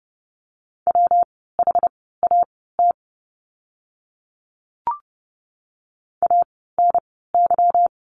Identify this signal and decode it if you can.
Morse code, heard as a keyed tone, about 30 wpm, spaced as for 17 wpm, 700 Hz. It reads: WHAT ANY
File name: course / 30:10wpm / 230 Lesson 37f - Letter x - Words 30wpm